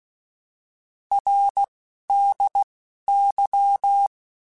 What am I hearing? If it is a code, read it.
Morse code, heard as a keyed tone, about 16 wpm, 800 Hz. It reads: R D Y
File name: Abreviations / Abreviations mp3s / Abreviations@10